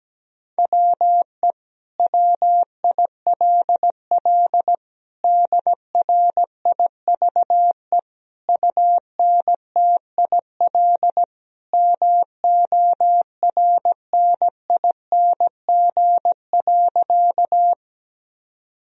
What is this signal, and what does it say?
Morse code, heard as a keyed tone, about 17 wpm, 700 Hz. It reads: WE WILL DRIVE UNTIL MORNING.
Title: Sentences from Top 1000 Words - Review Part 2 - 17wpm